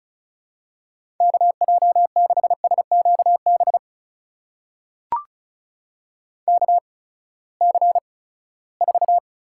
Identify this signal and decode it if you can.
Morse code, heard as a keyed tone, about 34 wpm, spaced as for 20 wpm, 700 Hz. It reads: KJ6SQB KC4
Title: USA Call Signs - 6 Characters - 8x Character Spacing - 35wpm